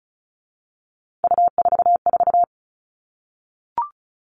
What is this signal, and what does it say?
Morse code, heard as a keyed tone, about 34 wpm, 700 Hz. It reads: U44
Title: All Cut Numbers - 3-Digit 35wpm